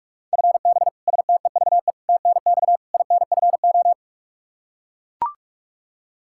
Morse code, heard as a keyed tone, about 43 wpm, 700 Hz. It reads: FB STEVE TNX INFO
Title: Practice QSO - Exchange 3 - More Info - 45wpm